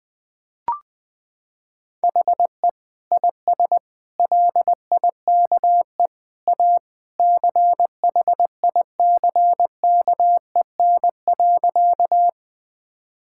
Morse code, heard as a keyed tone, about 21 wpm, 700 Hz. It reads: HE IS LIKE A CHICKEN.
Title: Sentences from Top 1000 Words - Review Part 5 - 20wpm